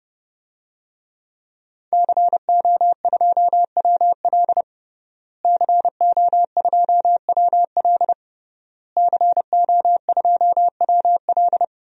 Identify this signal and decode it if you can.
Morse code, heard as a keyed tone, about 30 wpm, 700 Hz. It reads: CO2WL CO2WL CO2WL
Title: DXCC-100 Call Signs - 5 Characters - Repeated 3x - 30wpm